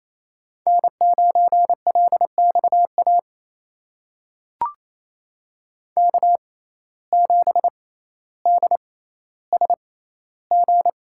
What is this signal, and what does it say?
Morse code, heard as a keyed tone, about 28 wpm, spaced as for 14 wpm, 700 Hz. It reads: N9LXA K7DSG
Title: USA Call Signs - 5 Characters - 6x Character Spacing - 28wpm